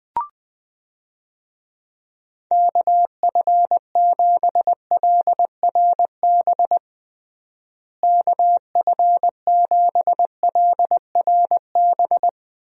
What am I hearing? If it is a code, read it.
Morse code, heard as a keyed tone, about 20 wpm, 700 Hz. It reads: KF7LRB KF7LRB
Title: USA Call Signs - 6 Characters - Repeated 2x - 20wpm